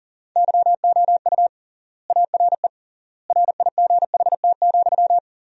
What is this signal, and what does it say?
Morse code, heard as a keyed tone, about 37 wpm, 700 Hz. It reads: YOU ARE RIGHT,
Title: 3x Word-Spacing - Sentences from Top 300 Words - 40wpm